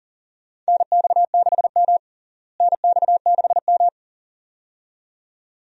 Morse code, heard as a keyed tone, about 37 wpm, 700 Hz. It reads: NX6M NX6M
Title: USA Call Signs - 4 Characters - Repeated 2x - 40wpm